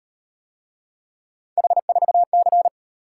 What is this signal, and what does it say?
Morse code, heard as a keyed tone, about 38 wpm, 700 Hz. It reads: H4C